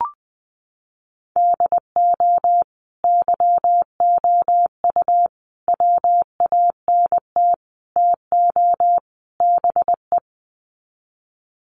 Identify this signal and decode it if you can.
Morse code, heard as a keyed tone, about 20 wpm, 700 Hz. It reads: DO YOU WANT TO BE
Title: Sets of 5 Words Encore2 20wpm